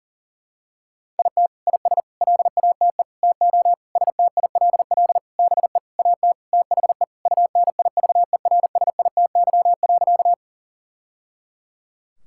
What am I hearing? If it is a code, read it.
Morse code, heard as a keyed tone, about 40 wpm, 700 Hz. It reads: IT IS LATE TO STILL BE AT THE UNIVERSITY.
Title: Sentences from Top 1000 Words - Review Part 5 - 40wpm